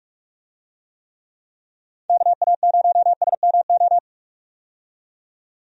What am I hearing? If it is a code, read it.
Morse code, heard as a keyed tone, about 45 wpm, 700 Hz. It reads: KA0SMO
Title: USA Call Signs - 6 Characters - 45wpm